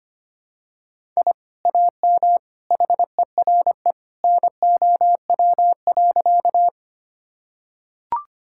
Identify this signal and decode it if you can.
Morse code, heard as a keyed tone, about 25 wpm, 700 Hz. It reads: I AM HERE NOW.
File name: practice / 25wpm / Sentences with 2-Word Phrases - 25wpm